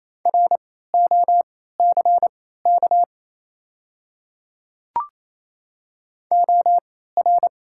Morse code, heard as a keyed tone, about 28 wpm, spaced as for 16 wpm, 700 Hz. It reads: ROCK OR
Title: 3x Character Spacing - Top 500 Words - 28wpm